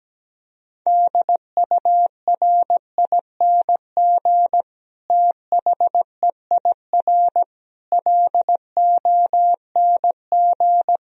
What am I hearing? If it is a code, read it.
Morse code, heard as a keyed tone, about 17 wpm, 700 Hz. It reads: DURING THEIR LONG
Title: Sentences from Top 1000 Words - Review Part 2 - 17wpm